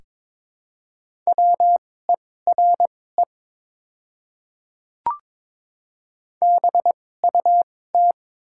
Morse code, heard as a keyed tone, about 22 wpm, spaced as for 16 wpm, 700 Hz. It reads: WERE BUT
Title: 2x Character Spacing - Top 100 Words - 22wpm